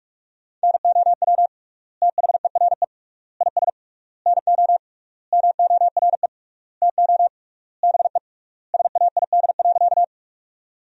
Morse code, heard as a keyed tone, about 46 wpm, 690 Hz. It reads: NOW THERE IS NO MORE TO BE SAID.
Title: 3x Word-Spacing - Sentences from Top 100 Words - 45wpm